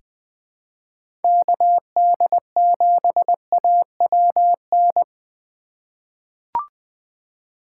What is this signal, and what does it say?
Morse code, heard as a keyed tone, about 20 wpm, 700 Hz. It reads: KD7AWN
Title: USA Call Signs - 6 Characters - 4x Character Spacing - 20wpm